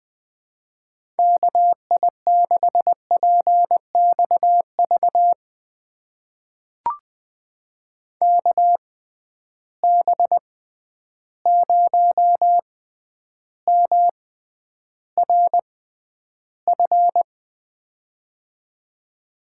Morse code, heard as a keyed tone, about 20 wpm, spaced as for 10 wpm, 700 Hz. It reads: KI6PXV KB0MRF
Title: USA Call Signs - 6 Characters - 6x Character Spacing - 20wpm